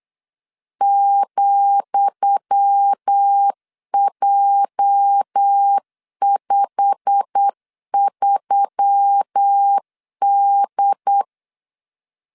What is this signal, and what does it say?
Morse code, heard as a keyed tone, about 8 wpm, 800 Hz. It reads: ,J53D
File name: LEZIONI / 10 SIGNS / 10D COMA 8wpm 3749